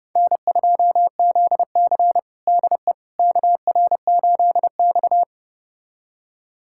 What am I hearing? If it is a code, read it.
Morse code, heard as a keyed tone, about 32 wpm, 700 Hz. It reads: N2ZC DE KR8X
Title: USA Callsign DE Callsign - 4 Characters - 30wpm